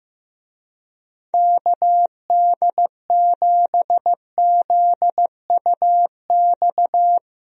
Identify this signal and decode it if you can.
Morse code, heard as a keyed tone, about 16 wpm, 700 Hz. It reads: KD7ZUX
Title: USA Call Signs - 6 Characters - Repeated 2x - 15wpm